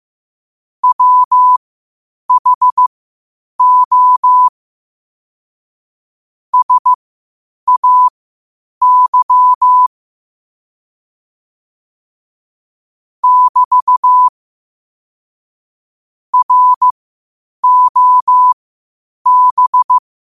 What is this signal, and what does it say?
Morse code, heard as a keyed tone, about 15 wpm, spaced as for 8 wpm, 1000 Hz. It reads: WHO SAY <BT> ROB
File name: qotdfiles / 8wpm